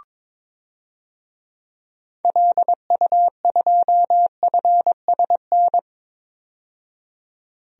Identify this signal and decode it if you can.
Morse code, heard as a keyed tone, about 22 wpm, 700 Hz. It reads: LU2FSN
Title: DXCC-100 Call Signs - 6 Characters - 22wpm